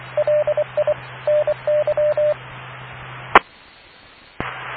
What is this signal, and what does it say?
Morse code, heard as a keyed tone, about 24 wpm, 590 Hz. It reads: LI NY E